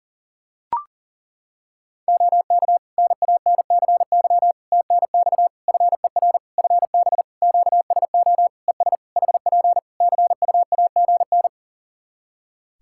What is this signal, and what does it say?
Morse code, heard as a keyed tone, about 37 wpm, 700 Hz. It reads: OK NANCY TNX FER FB QSO ES HP CUAGN
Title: Practice QSO - Exchange 4 - Ending - 40wpm